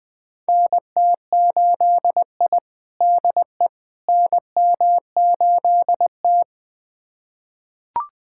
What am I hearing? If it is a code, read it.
Morse code, heard as a keyed tone, about 20 wpm, 700 Hz. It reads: NT8I DE NM8T
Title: USA Callsign DE Callsign - 4 Characters - 20wpm